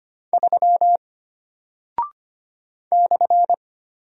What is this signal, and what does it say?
Morse code, heard as a keyed tone, about 25 wpm, 700 Hz. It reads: 3 /
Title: Single Character (Letter:Number:Punctuation:Prosign) Alt 25wpm